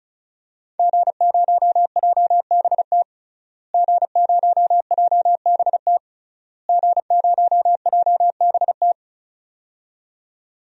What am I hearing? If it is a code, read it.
Morse code, heard as a keyed tone, about 34 wpm, 700 Hz. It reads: G0JBT G0JBT G0JBT
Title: DXCC-100 Call Signs - 5 Characters - Repeated 3x - 35wpm